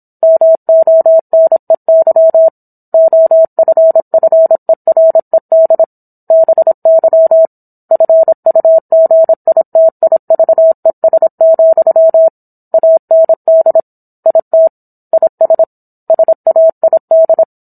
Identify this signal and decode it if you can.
Morse code, heard as a keyed tone, about 26 wpm, 650 Hz. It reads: MONEY OFFERED BY FUGITIVES, AND IT IS SAID